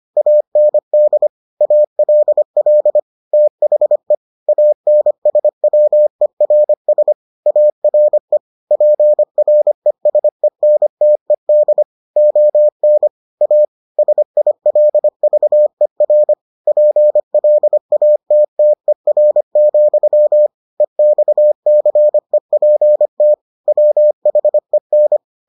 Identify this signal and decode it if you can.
Morse code, heard as a keyed tone, about 25 wpm, 600 Hz. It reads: AND ALL THE ANSWERS ARE PRESENTED ON A SILVER PLATTER, EXCEPT WHEN